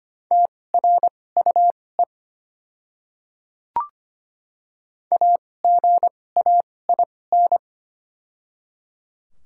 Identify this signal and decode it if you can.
Morse code, heard as a keyed tone, about 25 wpm, spaced as for 18 wpm, 700 Hz. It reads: TRUE AGAIN
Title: 2x Character Spacing - Top 500 Words - 25wpm